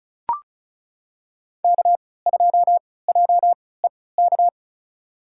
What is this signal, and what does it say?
Morse code, heard as a keyed tone, about 36 wpm, spaced as for 20 wpm, 700 Hz. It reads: K2JEK
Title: USA Call Signs - 5 Characters - 3x Character Spacing - 35wpm